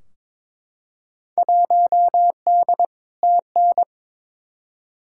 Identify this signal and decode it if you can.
Morse code, heard as a keyed tone, about 21 wpm, 700 Hz. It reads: 1D TN
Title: Field Day Exchanges - 22wpm